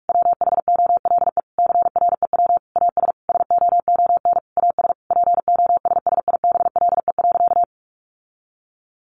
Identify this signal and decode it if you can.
Morse code, heard as a keyed tone, about 44 wpm, 700 Hz. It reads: WHOLE CREW AS SOON AS POSSIBLE.
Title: Sentences from Top 1500 Words 45wpm